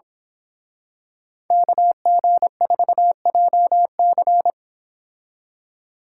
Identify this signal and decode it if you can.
Morse code, heard as a keyed tone, about 26 wpm, 700 Hz. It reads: KG4JC
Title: Speed Racing - USA 2x2 Call Signs 22wpm